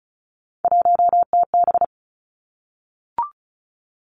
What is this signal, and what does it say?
Morse code, heard as a keyed tone, about 36 wpm, 700 Hz. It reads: 1TB
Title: Three Challenging Letter-Number Combinations 35wpm